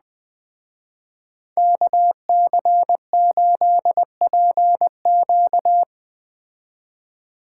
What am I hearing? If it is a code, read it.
Morse code, heard as a keyed tone, about 20 wpm, 700 Hz. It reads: KC8PQ
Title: Speed Racing - USA Call Signs 17wpm